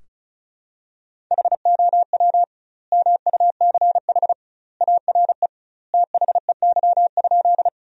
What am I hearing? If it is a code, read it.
Morse code, heard as a keyed tone, about 36 wpm, 700 Hz. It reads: HOW MUCH ARE THEY?
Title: 2x Word-Spacing - Sentences from Top 200 Words - 35wpm